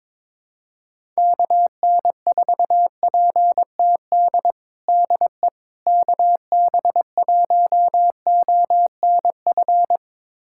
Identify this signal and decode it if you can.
Morse code, heard as a keyed tone, about 22 wpm, 700 Hz. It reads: KN4PTD DE KB1ONF
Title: USA Callsign DE Callsign - 6 Characters - 22wpm